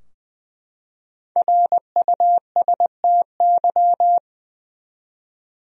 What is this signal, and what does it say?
Morse code, heard as a keyed tone, about 21 wpm, 700 Hz. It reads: RUSTY
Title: USA Common Names - Max 5 Letters - 20wpm